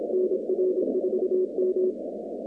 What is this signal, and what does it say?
Morse code, heard as a keyed tone, about 26 wpm, 370 Hz, with noise. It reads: NR4M
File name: 2019-cw-wpx-kt5j-k5pi